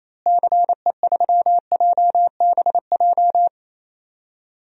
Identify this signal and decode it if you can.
Morse code, heard as a keyed tone, about 28 wpm, 700 Hz. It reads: CE3JBJ
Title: DXCC-100 Call Signs - 6 Characters - Repeated 2x - 28wpm